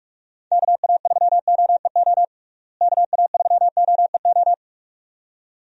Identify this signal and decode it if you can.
Morse code, heard as a keyed tone, about 46 wpm, 700 Hz. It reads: KA3OEO KA3OEO
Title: USA Call Signs - 6 Characters - Repeated 2x - 45wpm